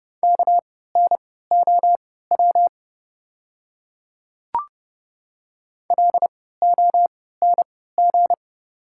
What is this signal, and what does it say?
Morse code, heard as a keyed tone, about 30 wpm, spaced as for 17 wpm, 700 Hz. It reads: KNOW LONG
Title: 3x Character Spacing - Top 100 Words - 30wpm